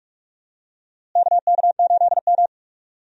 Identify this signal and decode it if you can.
Morse code, heard as a keyed tone, about 46 wpm, 700 Hz. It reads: KK8M
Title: USA Call Signs - 4 Characters - Repeated 3x - 45wpm